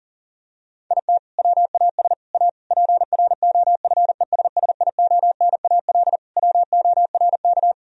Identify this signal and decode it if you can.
Morse code, heard as a keyed tone, about 40 wpm, 700 Hz. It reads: IT WAS A PROFESSIONAL WORK
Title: Sentences from Top 1000 Words - Review Part 4 - 40wpm